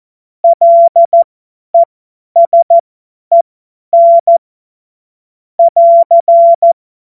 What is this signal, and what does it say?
Morse code, heard as a keyed tone, about 14 wpm, spaced as for 10 wpm, 680 Hz. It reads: LESEN <AR>
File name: Morsekurs / Witze / Witz 16. Mai 2023 50 F70